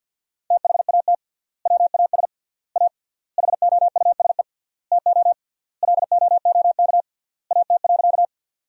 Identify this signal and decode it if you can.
Morse code, heard as a keyed tone, about 46 wpm, 700 Hz. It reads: THAT WAS A HOUSE TO LOOK AT.
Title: 3x Word-Spacing - Sentences from Top 200 Words - 50wpm